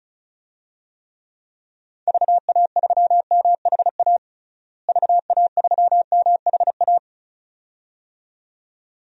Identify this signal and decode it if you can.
Morse code, heard as a keyed tone, about 36 wpm, 700 Hz. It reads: VA3MHA VA3MHA
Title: DXCC-100 Call Signs - 6 Characters - Repeated 2x - 35wpm